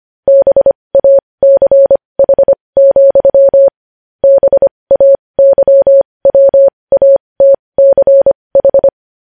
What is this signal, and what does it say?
Morse code, heard as a keyed tone, about 25 wpm, spaced as for 20 wpm, 550 Hz. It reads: BACH, BAYWATCH
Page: news-headlines-in-morse-code-at-20-wpm